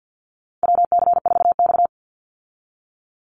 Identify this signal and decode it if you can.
Morse code, heard as a keyed tone, about 50 wpm, 700 Hz. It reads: WC4X
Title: Call Signs Repeated 2x - 50wpm